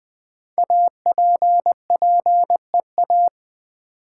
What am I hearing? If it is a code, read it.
Morse code, heard as a keyed tone, about 20 wpm, 700 Hz. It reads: APPEA
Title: Build A Word - 5 Letters At A Time - 10-Letter Words - 20wpm